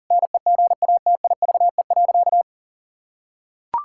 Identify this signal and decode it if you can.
Morse code, heard as a keyed tone, about 37 wpm, 700 Hz. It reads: NEGATIVE.
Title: Sentences from Top 1000 Words - Review Part 6 - 40wpm